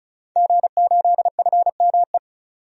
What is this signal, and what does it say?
Morse code, heard as a keyed tone, about 34 wpm, 700 Hz. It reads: G8FME
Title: DXCC-100 Call Signs - 5 Characters - 35wpm